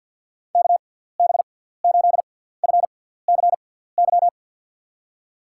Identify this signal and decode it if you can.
Morse code, heard as a keyed tone, about 53 wpm, spaced as for 17 wpm, 700 Hz. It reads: KB8FCY
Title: USA Call Signs - 6 Characters - 6x Character Spacing - 50wpm